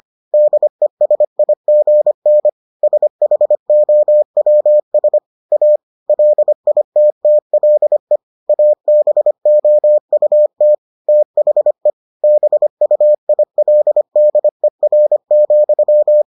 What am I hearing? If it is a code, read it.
Morse code, heard as a keyed tone, about 25 wpm, 600 Hz. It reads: DESIGN SHOWS A LITTLE ABOUT THE BUILDER,